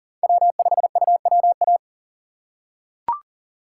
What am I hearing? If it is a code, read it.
Morse code, heard as a keyed tone, about 40 wpm, 700 Hz. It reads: W5UWA
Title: USA Call Signs - 5 Characters - 4x Character Spacing - 40wpm